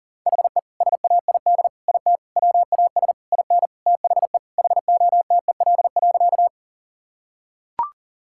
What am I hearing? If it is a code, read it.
Morse code, heard as a keyed tone, about 40 wpm, 700 Hz. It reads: HE SAID IT WAS IN THE HOTEL.
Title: Sentences from Top 1600 Words 40wpm